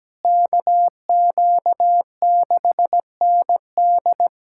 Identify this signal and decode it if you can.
Morse code, heard as a keyed tone, about 17 wpm, 700 Hz. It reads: KQ6ND
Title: USA Callsign DE Callsign - 5 Characters - 17wpm